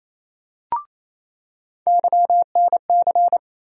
Morse code, heard as a keyed tone, about 28 wpm, 700 Hz. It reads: YNC
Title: USA Call Sign Postfixes - 3 Letters 28wpm